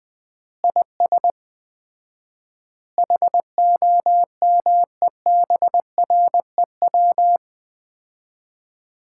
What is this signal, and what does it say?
Morse code, heard as a keyed tone, about 20 wpm, 700 Hz. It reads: IS HOMEBREW
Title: Common QSO Phrases - 4x Word-Spacing - 20wpm